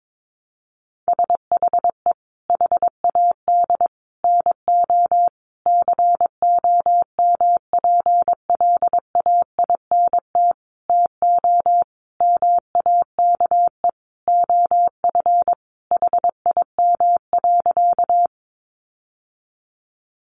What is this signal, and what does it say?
Morse code, heard as a keyed tone, about 22 wpm, 700 Hz. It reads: SHE HAD NO COMPLAINT TO MAKE OF HIM.
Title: Sentences from Top 1400 Words 22wpm